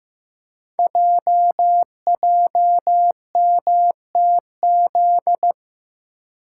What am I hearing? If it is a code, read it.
Morse code, heard as a keyed tone, about 15 wpm, 700 Hz. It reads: JJMTZ
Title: Code Groups - 5 Letters - 15wpm